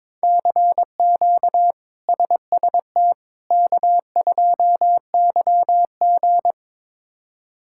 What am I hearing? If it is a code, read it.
Morse code, heard as a keyed tone, about 22 wpm, 700 Hz. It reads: CQ SST K2YG
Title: K1USN SST QSO Practice - 22wpm